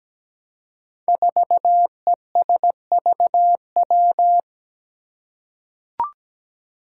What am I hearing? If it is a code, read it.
Morse code, heard as a keyed tone, about 17 wpm, 700 Hz. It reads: 4ESVW
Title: Five Challenging Letter-Number Combinations 17wpm